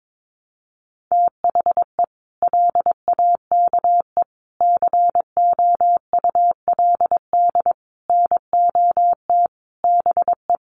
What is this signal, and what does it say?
Morse code, heard as a keyed tone, about 22 wpm, 700 Hz. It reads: THE LAKE COULD NOT BE
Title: Sentences from Top 1100 Words 22wpm